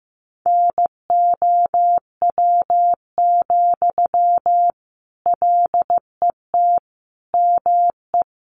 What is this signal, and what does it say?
Morse code, heard as a keyed tone, about 15 wpm, 700 Hz. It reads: NOW, LET ME
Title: Sentences from Top 800 Words 15wpm